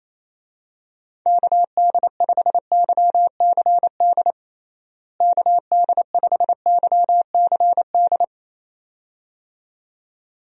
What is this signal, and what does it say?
Morse code, heard as a keyed tone, about 28 wpm, 700 Hz. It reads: KD5YCD KD5YCD
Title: USA Call Signs - 6 Characters - Repeated 2x - 28wpm